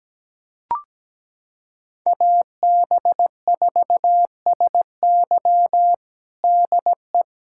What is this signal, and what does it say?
Morse code, heard as a keyed tone, about 17 wpm, 700 Hz. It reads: AB4SY DE
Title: USA Callsign DE Callsign - 5 Characters - 17wpm